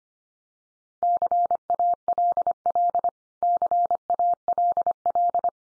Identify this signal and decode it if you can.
Morse code, heard as a keyed tone, about 25 wpm, 700 Hz. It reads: CALL CALL
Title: Top 100 Words 25wpm x3